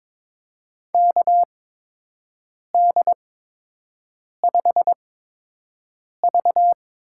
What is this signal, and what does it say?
Morse code, heard as a keyed tone, about 22 wpm, spaced as for 6 wpm, 700 Hz. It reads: KD5V
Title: USA Call Signs - 6 Characters - 8x Character Spacing - 22wpm